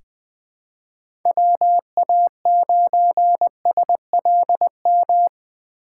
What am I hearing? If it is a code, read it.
Morse code, heard as a keyed tone, about 20 wpm, 700 Hz. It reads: WA9SLM
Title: USA Call Signs - 6 Characters - 4x Character Spacing - 20wpm